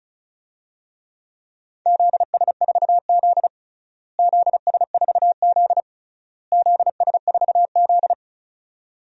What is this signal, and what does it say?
Morse code, heard as a keyed tone, about 36 wpm, 690 Hz. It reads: ZS4Z ZS4Z ZS4Z
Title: DXCC-100 Call Signs - 4 Characters - Repeated 3x - 35wpm